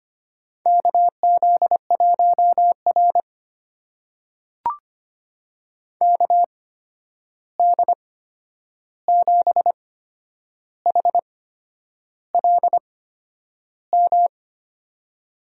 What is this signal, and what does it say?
Morse code, heard as a keyed tone, about 25 wpm, spaced as for 9 wpm, 700 Hz. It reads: KZ1R KD7HLM
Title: USA Call Signs - 8x Character Spacing - 25wpm